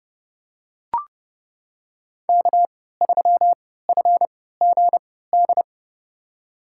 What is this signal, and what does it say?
Morse code, heard as a keyed tone, about 30 wpm, spaced as for 17 wpm, 700 Hz. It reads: K3FGD
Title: USA Call Signs - 5 Characters - 3x Character Spacing - 30wpm